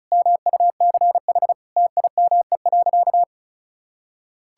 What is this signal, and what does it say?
Morse code, heard as a keyed tone, about 36 wpm, 700 Hz. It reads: MUCH TIME.
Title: Sentences from Top 1000 Words - Review Part 6 - 35wpm